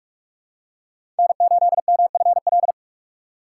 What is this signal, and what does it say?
Morse code, heard as a keyed tone, about 46 wpm, 700 Hz. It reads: N8MFL
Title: USA Call Signs - 3x Character Spacing - 45wpm